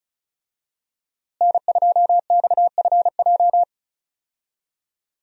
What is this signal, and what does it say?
Morse code, heard as a keyed tone, about 36 wpm, 700 Hz. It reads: N2XFJ
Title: USA Call Signs - 5 Characters - Repeated 3x - 35wpm